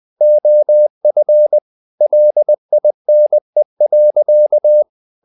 Morse code, heard as a keyed tone, about 20 wpm, 600 Hz. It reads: OF LINE.
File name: the old man and the sea / the old man and the sea part 1 20wpm